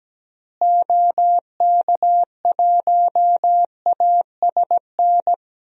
Morse code, heard as a keyed tone, about 17 wpm, 700 Hz. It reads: OK1ASN
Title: DXCC-100 Call Signs - 6 Characters - Repeated 3x - 17wpm